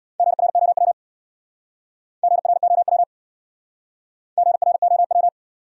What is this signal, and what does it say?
Morse code, heard as a keyed tone, about 69 wpm, 700 Hz. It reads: GROW GROW GROW